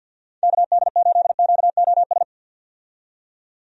Morse code, heard as a keyed tone, about 50 wpm, 700 Hz. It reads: KD8QOS
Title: USA Call Signs - 6 Characters - 50wpm